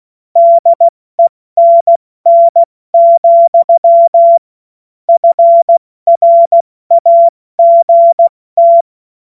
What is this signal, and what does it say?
Morse code, heard as a keyed tone, about 16 wpm, 680 Hz. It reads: DENN, FRAGT
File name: Morsekurs / Witze / Witz 20. Januar 2026 70 F80